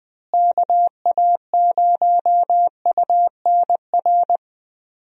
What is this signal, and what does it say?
Morse code, heard as a keyed tone, about 20 wpm, 700 Hz. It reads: KA0UNR
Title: USA Call Signs - 6 Characters - Repeated 2x - 20wpm